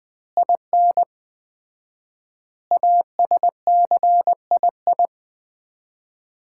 Morse code, heard as a keyed tone, about 20 wpm, 700 Hz. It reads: IN ASCII
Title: 4x Word-Spacing - Sets of 5 Words - 20wpm